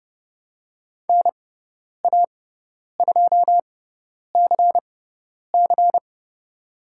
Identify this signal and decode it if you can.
Morse code, heard as a keyed tone, about 30 wpm, spaced as for 10 wpm, 700 Hz. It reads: NA2CC
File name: Lesson 18g - Letter m - Call Signs 30wpm